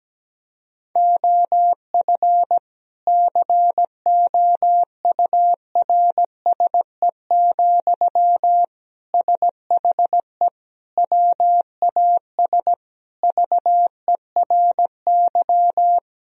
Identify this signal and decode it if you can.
Morse code, heard as a keyed tone, about 17 wpm, 700 Hz. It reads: OF COURSE, SHE WAS VERY
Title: Sentences from Top 1000 Words - Review Part 4 - 17wpm